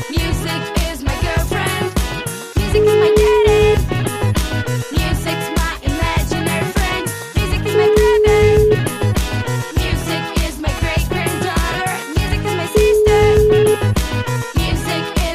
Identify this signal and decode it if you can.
Ringback tone (Europe, 425 Hz), with noise.